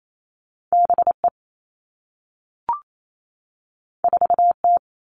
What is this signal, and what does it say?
Morse code, heard as a keyed tone, about 28 wpm, 700 Hz. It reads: B E 4 T
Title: Two Challenging Letter-Number Combinations 28wpm